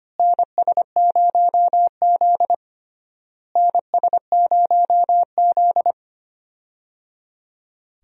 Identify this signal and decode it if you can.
Morse code, heard as a keyed tone, about 25 wpm, 700 Hz. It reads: NS0Z NS0Z
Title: USA Call Signs - 4 Characters - Repeated 3x - 25wpm